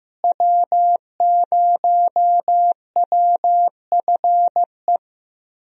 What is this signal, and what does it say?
Morse code, heard as a keyed tone, about 15 wpm, 700 Hz. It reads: W0WFE
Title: USA Call Signs - 5 Characters - Repeated 3x - 15wpm